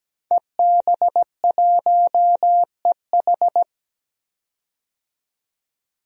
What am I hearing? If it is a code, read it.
Morse code, heard as a keyed tone, about 17 wpm, 700 Hz. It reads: EB1EH
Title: DXCC-100 Call Signs - 5 Characters - 17wpm